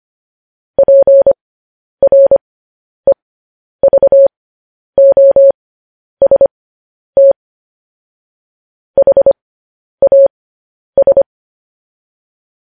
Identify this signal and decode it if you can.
Morse code, heard as a keyed tone, about 25 wpm, spaced as for 10 wpm, 550 Hz. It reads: PREVOST HAS
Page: news-headlines-in-morse-code-at-10-wpm